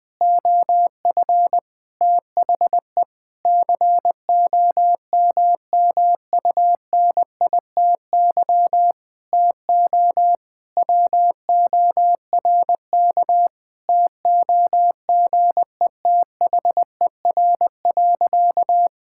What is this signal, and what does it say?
Morse code, heard as a keyed tone, about 20 wpm, 700 Hz. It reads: OF THE COMMUNITY TO WORK TOGETHER.
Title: Sentences with 7-Word Phrases - 20wpm